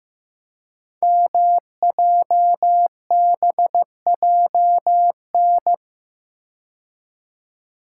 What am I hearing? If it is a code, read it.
Morse code, heard as a keyed tone, about 15 wpm, 700 Hz. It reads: MJBJN
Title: Code Groups - 5 Letters - 15wpm